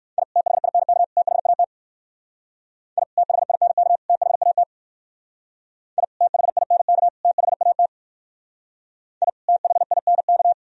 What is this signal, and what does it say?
Morse code, heard as a keyed tone, about 60 wpm, 700 Hz. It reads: I THINK THAT I THINK THAT I THINK THAT I THINK